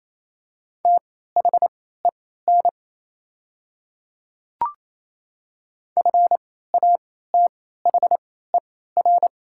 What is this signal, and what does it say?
Morse code, heard as a keyed tone, about 28 wpm, spaced as for 16 wpm, 690 Hz. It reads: THEN FATHER
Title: 3x Character Spacing - Top 500 Words - 28wpm